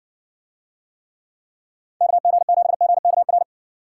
Morse code, heard as a keyed tone, about 60 wpm, 700 Hz. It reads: KG7GKR